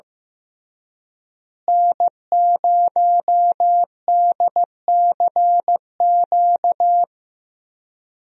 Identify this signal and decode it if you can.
Morse code, heard as a keyed tone, about 15 wpm, 700 Hz. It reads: N0DCQ
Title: USA Call Signs - 5 Characters - Repeated 3x - 15wpm